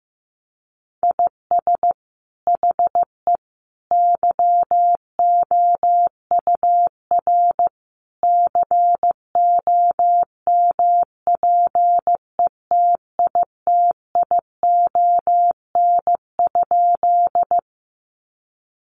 Morse code, heard as a keyed tone, about 15 wpm, 700 Hz. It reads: IS HE YOUR COMPETITION?
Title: Sentences from Top 1000 Words 15wpm